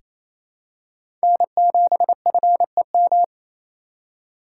Morse code, heard as a keyed tone, about 28 wpm, 700 Hz. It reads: N7FEM
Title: USA Call Signs - 5 Characters - 2x Character Spacing - 28wpm